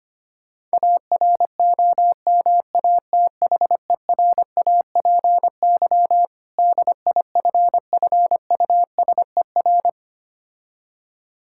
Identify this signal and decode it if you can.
Morse code, heard as a keyed tone, about 25 wpm, 700 Hz. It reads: AROMATHERAPY DIFFUSER
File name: practice / 25wpm / Difficult Words with Hints - 25wpm